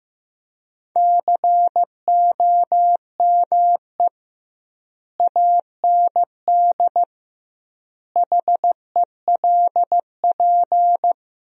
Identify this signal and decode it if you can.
Morse code, heard as a keyed tone, about 15 wpm, 700 Hz. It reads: COME AND HELP
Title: 2x Word-Spacing - Sentences from Top 200 Words - 15wpm